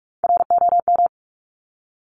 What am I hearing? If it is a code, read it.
Morse code, heard as a keyed tone, about 46 wpm, 700 Hz. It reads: ROM
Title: Top Three Letter Combinations in Words 45wpm